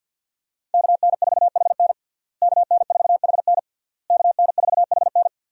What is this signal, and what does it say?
Morse code, heard as a keyed tone, about 52 wpm, 700 Hz. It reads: KN4HN KN4HN KN4HN
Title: USA Call Signs - 5 Characters - Repeated 3x - 50wpm